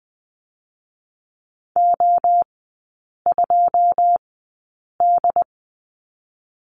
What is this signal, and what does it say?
Morse code, heard as a keyed tone, about 20 wpm, spaced as for 8 wpm, 700 Hz. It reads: O2D
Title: 081 Lesson 14e - Number 2 - Three Characters 20wpm